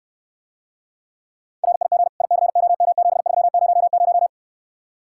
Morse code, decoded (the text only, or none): RIG ICOM7300